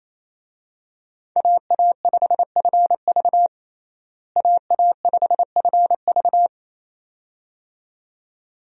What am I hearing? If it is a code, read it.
Morse code, heard as a keyed tone, about 28 wpm, 700 Hz. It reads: AA5FV AA5FV
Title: USA Call Signs - 5 Characters - Repeated 2x - 28wpm